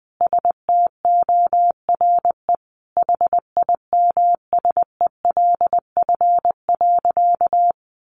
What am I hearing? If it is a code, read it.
Morse code, heard as a keyed tone, about 20 wpm, 700 Hz. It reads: STORE HIMSELF.
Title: Additional Easy Phrases 20wpm